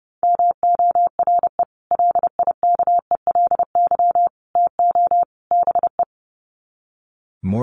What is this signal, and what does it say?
Morse code, heard as a keyed tone, about 30 wpm, 700 Hz. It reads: MORE LIKELY TO BE
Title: Sets of 4 Words 30wpm